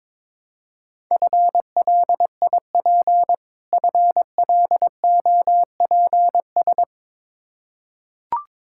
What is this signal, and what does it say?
Morse code, heard as a keyed tone, about 22 wpm, 700 Hz. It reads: FLIP FLOPS
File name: practice / 22wpm / Difficult Words with Hints - 22wpm